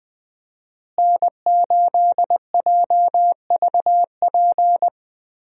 Morse code, heard as a keyed tone, about 20 wpm, 700 Hz. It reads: N8JVP
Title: USA Call Signs - 5 Characters - Repeated 2x - 20wpm